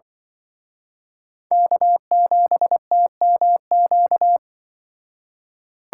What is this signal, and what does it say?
Morse code, heard as a keyed tone, about 24 wpm, 700 Hz. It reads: K7TMQ
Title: Speed Racing - USA 1x3 Call Signs 20wpm